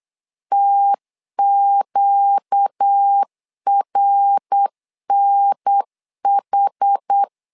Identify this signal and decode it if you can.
Morse code, heard as a keyed tone, about 8 wpm, 800 Hz. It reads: TQRNH
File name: 4 Q 8wpm